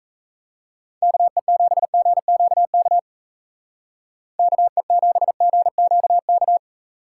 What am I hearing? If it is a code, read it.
Morse code, heard as a keyed tone, about 36 wpm, 700 Hz. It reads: KE7GQK KE7GQK